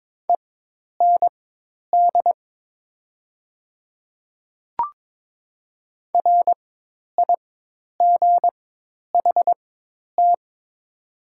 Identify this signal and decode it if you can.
Morse code, heard as a keyed tone, about 22 wpm, spaced as for 10 wpm, 700 Hz. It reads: END RIGHT
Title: 4x Character Spacing - Top 200 Words - 22wpm